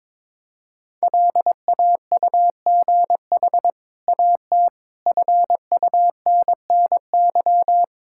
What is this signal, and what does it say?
Morse code, heard as a keyed tone, about 22 wpm, 700 Hz. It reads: LAUGH AT FUNNY
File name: practice / 22wpm / Easy Sentences with Hints - 22wpm